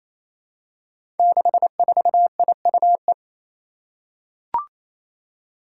Morse code, heard as a keyed tone, about 29 wpm, 700 Hz. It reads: 64IUE
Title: Five Challenging Letter-Number Combinations 28wpm